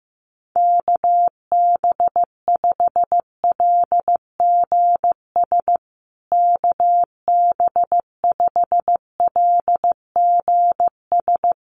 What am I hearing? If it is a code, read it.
Morse code, heard as a keyed tone, about 15 wpm, 700 Hz. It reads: KB5LGS KB5LGS
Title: USA Call Signs Repeated 3x 15wpm